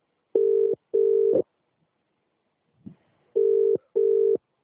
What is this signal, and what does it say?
Ringback tone (UK), 400 + 450 Hz.